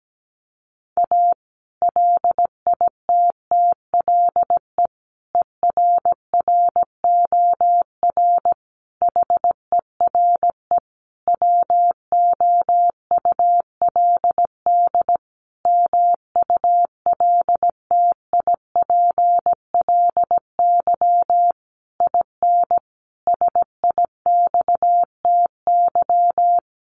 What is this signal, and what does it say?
Morse code, heard as a keyed tone, about 17 wpm, 700 Hz. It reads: A LITTLE ERROR HERE WOULD MULTIPLY IN SIXTY
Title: A Sound Of Thunder 17wpm